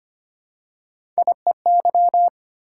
Morse code, heard as a keyed tone, about 25 wpm, 700 Hz. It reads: IEY
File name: Three Letter-Number Combinations - 25wpm